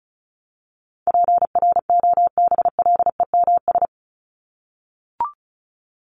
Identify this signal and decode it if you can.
Morse code, heard as a keyed tone, about 34 wpm, 700 Hz. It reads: PROBLEMS
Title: ES:S Suffix - Top 100 Words with Base Words 35wpm